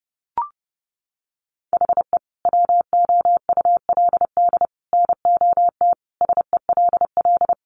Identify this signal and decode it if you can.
Morse code, heard as a keyed tone, about 30 wpm, 700 Hz. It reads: HE WOULD NOT SELL
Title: Sentences from Top 1100 Words 30wpm